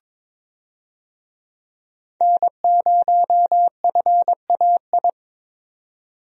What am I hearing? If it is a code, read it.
Morse code, heard as a keyed tone, about 22 wpm, 700 Hz. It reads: N0FAI
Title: USA Call Signs - 5 Characters - Repeated 3x - 22wpm